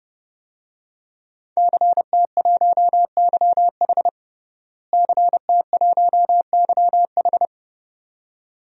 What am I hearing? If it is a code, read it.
Morse code, heard as a keyed tone, about 30 wpm, 700 Hz. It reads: CT1YH CT1YH
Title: DXCC-100 Call Signs - 5 Characters - Repeated 2x - 30wpm